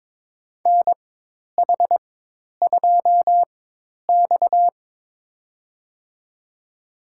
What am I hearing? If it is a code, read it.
Morse code, heard as a keyed tone, about 21 wpm, spaced as for 10 wpm, 700 Hz. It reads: NH2X
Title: USA Call Signs - 4 Characters - 4x Character Spacing - 22wpm